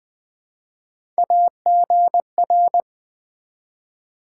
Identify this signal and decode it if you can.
Morse code, heard as a keyed tone, about 20 wpm, 700 Hz. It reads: AGR
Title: Build A Word - 1 Letter At A Time - 5-Letter Words - 20wpm